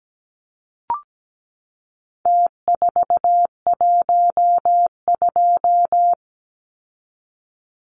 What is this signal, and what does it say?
Morse code, heard as a keyed tone, about 17 wpm, 700 Hz. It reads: T412
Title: Numbers - Contest Style - 4-Digit 17wpm